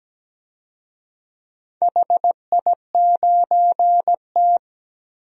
Morse code, heard as a keyed tone, about 17 wpm, 700 Hz. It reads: HI9T
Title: DXCC-100 Call Signs - 4 Characters - 17wpm